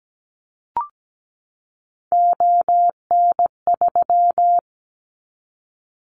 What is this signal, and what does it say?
Morse code, heard as a keyed tone, about 17 wpm, 700 Hz. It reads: ON3
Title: Callsign Prefixes for Top 25 DXCC - 17wpm